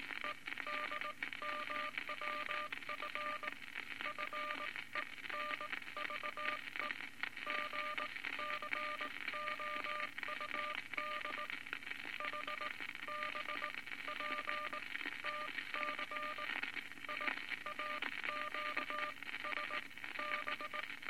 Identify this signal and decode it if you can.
Morse code, heard as a keyed tone, about 19 wpm, spaced as for 15 wpm, 1200 Hz, with noise. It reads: EDMWF FENVE GCOUD HBPTC IAQSB
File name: Morse Code Records / 19 B LCM 167 Twelve Words Per Minute